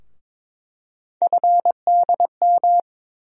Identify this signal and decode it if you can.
Morse code, heard as a keyed tone, about 22 wpm, 700 Hz. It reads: FDM
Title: USA Call Sign Postfixes - 3 Letters 22wpm